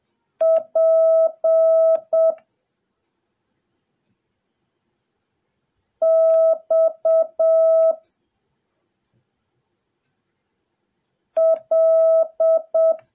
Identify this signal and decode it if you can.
Morse code, heard as a keyed tone, about 7 wpm, 650 Hz. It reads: P X L
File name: jh8cbh / 2021cw / cw08a